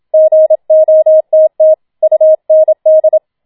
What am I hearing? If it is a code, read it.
Morse code, heard as a keyed tone, about 27 wpm, 630 Hz. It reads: GOTT UND
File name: Schneewittchen Teil2 27wpm